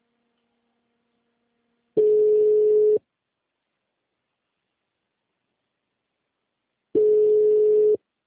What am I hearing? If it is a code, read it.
Ringback tone (Europe, 425 Hz).